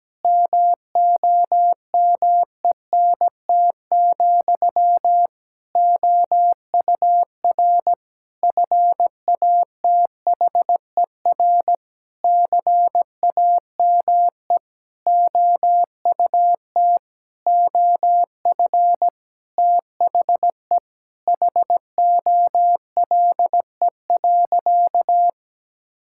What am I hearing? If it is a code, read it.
Morse code, heard as a keyed tone, about 17 wpm, 700 Hz. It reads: MOMENT, OUR FATHER CAME OUT OF THE HOLE.
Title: Sentences from Top 1600 Words 17wpm